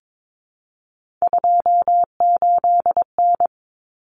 Morse code, heard as a keyed tone, about 22 wpm, 700 Hz. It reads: 28N
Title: Common Cut Numbers - 3-Digit 22wpm